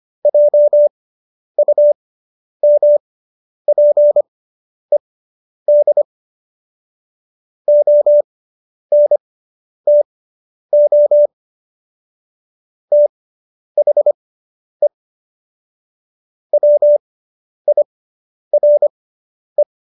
Morse code, heard as a keyed tone, about 25 wpm, spaced as for 10 wpm, 590 Hz. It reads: JUMPED ONTO THE WIRE